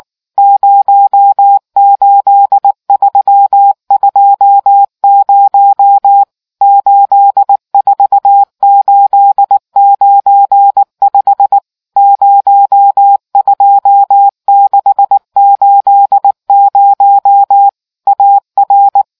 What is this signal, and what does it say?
Morse code, heard as a keyed tone, about 19 wpm, 800 Hz. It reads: 08320 84895 02680 AR